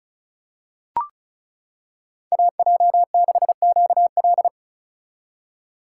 Morse code, heard as a keyed tone, about 34 wpm, 700 Hz. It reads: AJ6QL
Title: K1USN SST QSO Practice - 35wpm